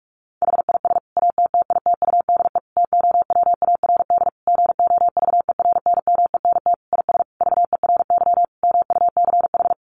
Morse code, heard as a keyed tone, about 44 wpm, 700 Hz. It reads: HIS ATTITUDE TOWARD GOVERNMENT IS VERY MUCH
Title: Sentences from Top 1300 Words 45wpm